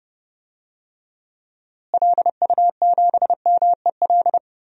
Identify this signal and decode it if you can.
Morse code, heard as a keyed tone, about 29 wpm, 700 Hz. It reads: LU7MEL
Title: DXCC-100 Call Signs - 6 Characters - Repeated 3x - 30wpm